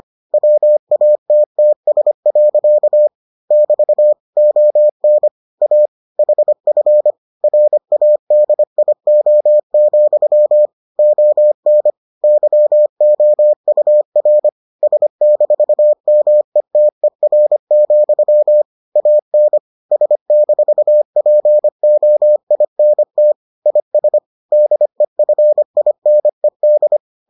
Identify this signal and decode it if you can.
Morse code, heard as a keyed tone, about 25 wpm, 600 Hz. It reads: WATTS. <BT> ON A HF RADIO, ON YOUR S-METER, AN S-POINT IS DEFINED